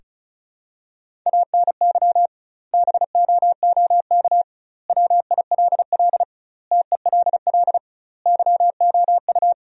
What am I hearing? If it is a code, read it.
Morse code, heard as a keyed tone, about 34 wpm, 700 Hz. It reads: ANY BOOK WILL TELL YOU
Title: 2x Word-Spacing - Sentences from Top 300 Words - 35wpm